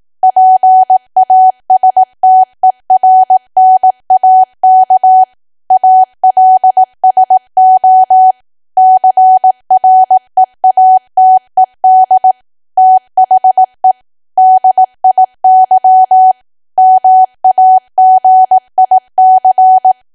Morse code, heard as a keyed tone, about 18 wpm, 750 Hz. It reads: PASTERNAK ALSO CREATED THE DIY MAGIC
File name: Archive / 18 WPM / 18WPM